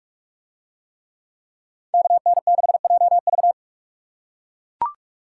Morse code, heard as a keyed tone, about 42 wpm, 700 Hz. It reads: KN6JV